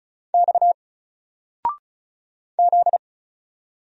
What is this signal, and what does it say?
Morse code, heard as a keyed tone, about 36 wpm, 700 Hz. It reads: X Z
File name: Single-Letter-Number-35wpm